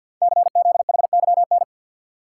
Morse code, heard as a keyed tone, about 52 wpm, 690 Hz. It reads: CZHYN
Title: Code Groups - 5 Letters - 50wpm